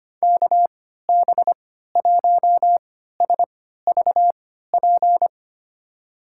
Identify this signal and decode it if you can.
Morse code, heard as a keyed tone, about 25 wpm, spaced as for 14 wpm, 700 Hz. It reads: KB1SVP
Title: USA Call Signs - 3x Character Spacing - 25wpm